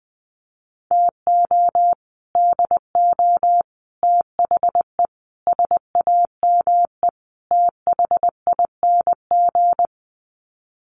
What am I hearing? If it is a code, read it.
Morse code, heard as a keyed tone, about 20 wpm, 700 Hz. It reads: TO DO THE SAME THING
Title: Sets of 5 Words Encore 20wpm